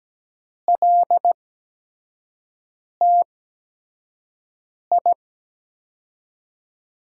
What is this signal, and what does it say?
Morse code, heard as a keyed tone, about 17 wpm, spaced as for 5 wpm, 700 Hz. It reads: LTI
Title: USA Call Signs - 5 Characters - 8x Character Spacing - 17wpm